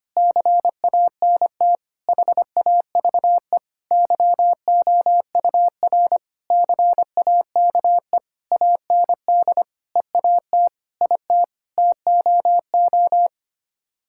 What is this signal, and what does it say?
Morse code, heard as a keyed tone, about 25 wpm, 700 Hz. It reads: CANT HAVE YOUR CAKE AND EAT IT TOO
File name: Sets of 9 Words - 25wpm